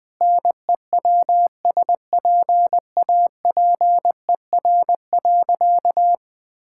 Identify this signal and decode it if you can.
Morse code, heard as a keyed tone, about 20 wpm, 700 Hz. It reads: NEWSPAPER.
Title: Sentences from Top 1000 Words - Review Part 6 - 20wpm